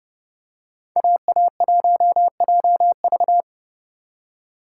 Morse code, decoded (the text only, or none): AA1JV